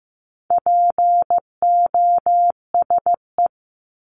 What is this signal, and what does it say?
Morse code, heard as a keyed tone, about 15 wpm, 700 Hz. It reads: POSE
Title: Top 500 Words 15wpm